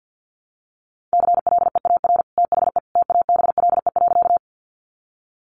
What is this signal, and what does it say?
Morse code, heard as a keyed tone, about 50 wpm, 700 Hz. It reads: CLEAR THE TABLE.
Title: Sentences from Top 500 Words 50wpm